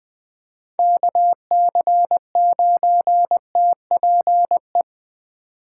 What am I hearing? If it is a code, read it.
Morse code, heard as a keyed tone, about 24 wpm, 700 Hz. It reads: KC9TPE
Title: Speed Racing - USA Call Signs 20wpm